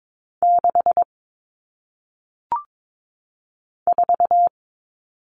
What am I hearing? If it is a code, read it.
Morse code, heard as a keyed tone, about 22 wpm, 700 Hz. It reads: 6 4
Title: Numbers - 1-Digit 22wpm